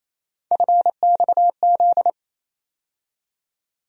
Morse code, heard as a keyed tone, about 28 wpm, 700 Hz. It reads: FXZ
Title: USA Call Sign Postfixes - 3 Letters 28wpm